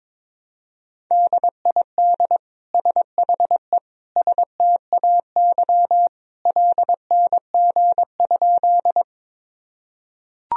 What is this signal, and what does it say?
Morse code, heard as a keyed tone, about 21 wpm, 700 Hz. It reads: DID SHE STAY LNG?
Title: Sentences from Top 500 Words and Abbreviations 22wpm